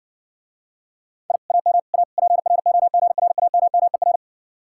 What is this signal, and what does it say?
Morse code, heard as a keyed tone, about 60 wpm, 700 Hz. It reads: I AM A PROGRAMMER